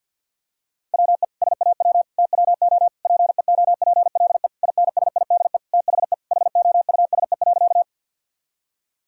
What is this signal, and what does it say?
Morse code, heard as a keyed tone, about 50 wpm, 700 Hz. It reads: WE SAW TWO PEOPLE INSIDE THE HOUSE.
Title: Sentences from Top 1000 Words - Review Part 6 - 50wpm